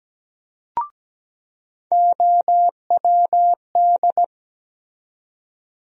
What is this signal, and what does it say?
Morse code, heard as a keyed tone, about 17 wpm, 700 Hz. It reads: OWD
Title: USA Call Sign Postfixes - 3 Letters 17wpm